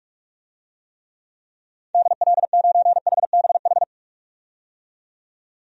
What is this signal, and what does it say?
Morse code, heard as a keyed tone, about 46 wpm, 700 Hz. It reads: DL9HBH
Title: DXCC-100 Call Signs - 6 Characters - 45wpm